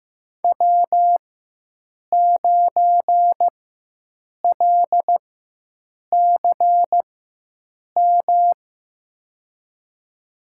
Morse code, heard as a keyed tone, about 15 wpm, spaced as for 7 wpm, 700 Hz. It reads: W9LCM
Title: USA Call Signs - 5 Characters - 4x Character Spacing - 15wpm